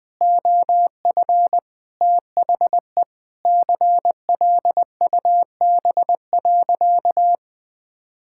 Morse code, heard as a keyed tone, about 21 wpm, 700 Hz. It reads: OF THE CLUB.
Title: Sentences with 5-Word Phrases - 20wpm